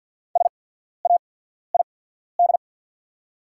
Morse code, heard as a keyed tone, about 50 wpm, spaced as for 14 wpm, 700 Hz. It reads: SAID
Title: 8x Character Spacing - Top 100 Words - 50wpm